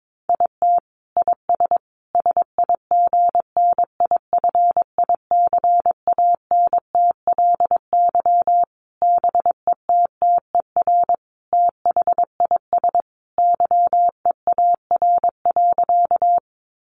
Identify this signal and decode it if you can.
Morse code, heard as a keyed tone, about 22 wpm, 700 Hz. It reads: IT IS SIGNIFICANTLY BETTER THIS YEAR.
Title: Sentences from Top 1500 Words 22wpm